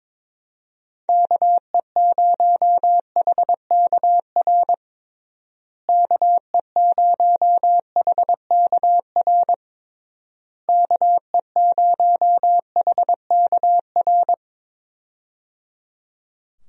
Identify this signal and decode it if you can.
Morse code, heard as a keyed tone, about 22 wpm, 700 Hz. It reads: KE0HKR KE0HKR KE0HKR
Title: USA Call Signs - 6 Characters - Repeated 3x - 22wpm